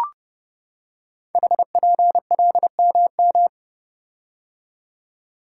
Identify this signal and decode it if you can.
Morse code, heard as a keyed tone, about 28 wpm, 690 Hz. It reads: HPLMM
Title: Code Groups - 5 Letters - 30wpm